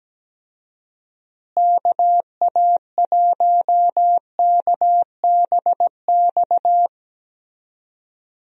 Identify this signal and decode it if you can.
Morse code, heard as a keyed tone, about 17 wpm, 700 Hz. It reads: KA1KBX
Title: USA Call Signs - 6 Characters - Repeated 2x - 17wpm